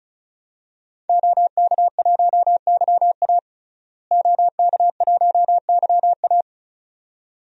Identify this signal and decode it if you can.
Morse code, heard as a keyed tone, about 36 wpm, 700 Hz. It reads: OK1YA OK1YA
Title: DXCC-100 Call Signs - 5 Characters - Repeated 2x - 35wpm